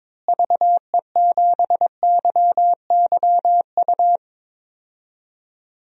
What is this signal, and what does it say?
Morse code, heard as a keyed tone, about 22 wpm, 700 Hz. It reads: VE7YYU
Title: DXCC-100 Call Signs - 6 Characters - Repeated 2x - 22wpm